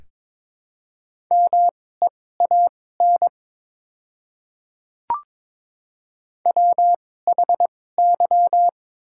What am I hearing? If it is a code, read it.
Morse code, heard as a keyed tone, about 22 wpm, spaced as for 16 wpm, 700 Hz. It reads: MEAN WHY
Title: 2x Character Spacing - Top 200 Words - 22wpm